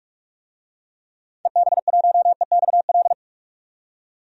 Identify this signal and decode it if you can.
Morse code, heard as a keyed tone, about 41 wpm, 700 Hz. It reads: EB1EXL
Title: DXCC-100 Call Signs - 6 Characters - 45wpm